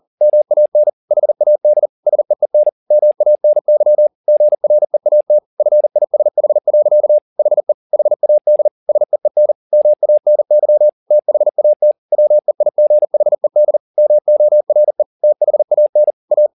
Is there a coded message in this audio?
Yes: Morse code from 0.2 to 16.5 s, heard as a keyed tone, about 40 wpm, 600 Hz. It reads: MAN HAD SEEN MANY GREAT FISH. HE HAD SEEN MANY THAT WEIGHED MORE THAN A